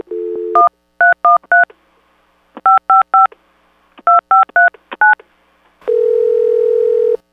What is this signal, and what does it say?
DTMF, heard as telephone touch tones, tones of about 150 ms. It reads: 1313 555 2539
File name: Random PLA Sound Clips / rbcp prank calling eminem